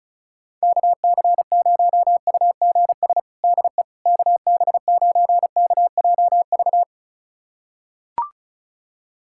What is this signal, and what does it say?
Morse code, heard as a keyed tone, about 34 wpm, 700 Hz. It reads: KC0UGS DE KB9KJV
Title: USA Callsign DE Callsign - 6 Characters - 35wpm